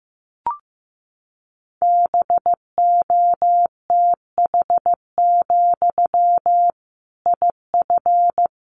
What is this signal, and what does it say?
Morse code, heard as a keyed tone, about 15 wpm, 700 Hz. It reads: BOTH, IF
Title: Sentences from Top 1500 Words 15wpm